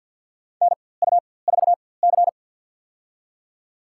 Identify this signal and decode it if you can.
Morse code, heard as a keyed tone, about 46 wpm, spaced as for 23 wpm, 700 Hz. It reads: NU4C
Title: USA Call Signs - 4x Character Spacing - 50wpm